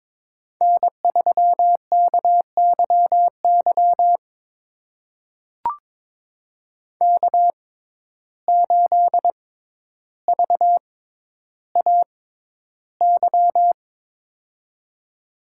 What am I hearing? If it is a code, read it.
Morse code, heard as a keyed tone, about 22 wpm, spaced as for 11 wpm, 700 Hz. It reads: N3KYY K8VAY
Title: USA Call Signs - 6x Character Spacing - 22wpm